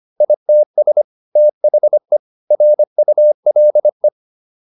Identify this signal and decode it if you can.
Morse code, heard as a keyed tone, about 25 wpm, 600 Hz. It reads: ITS THE RULE